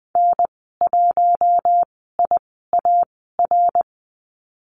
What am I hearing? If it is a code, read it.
Morse code, heard as a keyed tone, about 20 wpm, spaced as for 14 wpm, 700 Hz. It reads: N1IAR
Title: 069 Lesson 12g - Letter d - Call Signs 20wpm